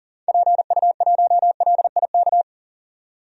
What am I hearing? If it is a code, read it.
Morse code, heard as a keyed tone, about 40 wpm, 700 Hz. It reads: PU1LIK
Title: DXCC-100 Call Signs - 6 Characters - 40wpm